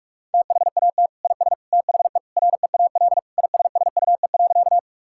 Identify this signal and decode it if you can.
Morse code, heard as a keyed tone, about 46 wpm, 700 Hz. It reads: THAT IS THE REAL ISSUE.
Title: Sentences from Top 1000 Words - Review Part 6 - 45wpm